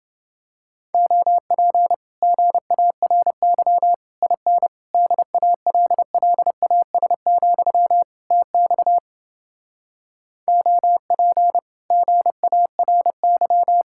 Morse code, heard as a keyed tone, about 28 wpm, 700 Hz. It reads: OP GARY IN DALLAS, TX OP GARY